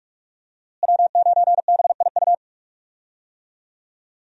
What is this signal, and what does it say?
Morse code, heard as a keyed tone, about 43 wpm, 700 Hz. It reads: W9BIU
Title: USA Call Signs - 5 Characters - Repeated 3x - 45wpm